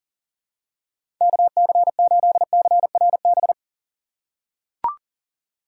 Morse code, heard as a keyed tone, about 40 wpm, 700 Hz. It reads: KC8CRB
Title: USA Call Signs - 6 Characters - 4x Character Spacing - 40wpm